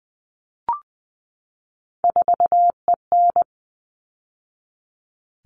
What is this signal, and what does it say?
Morse code, heard as a keyed tone, about 20 wpm, 700 Hz. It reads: 4EN
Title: All Cut Numbers - 3-Digit 20wpm